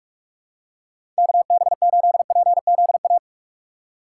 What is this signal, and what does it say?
Morse code, heard as a keyed tone, about 44 wpm, 700 Hz. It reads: K B 8 P Z A
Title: USA Call Signs - 6 Characters - 6x Character Spacing - 45wpm